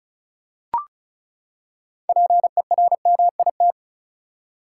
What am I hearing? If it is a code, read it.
Morse code, heard as a keyed tone, about 36 wpm, 700 Hz. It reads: PERMIT
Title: Six Letter Words - 35wpm